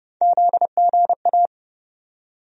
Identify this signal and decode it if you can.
Morse code, heard as a keyed tone, about 31 wpm, 700 Hz. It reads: ZGA
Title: USA Call Sign Postfixes - 3 Letters 30wpm